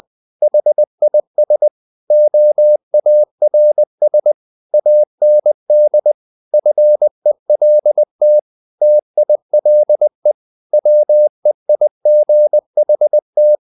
Morse code, heard as a keyed tone, about 20 wpm, 600 Hz. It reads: HIS OARS AND FELT TILE WEIGHT
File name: the old man and the sea / the old man and the sea part 1 20wpm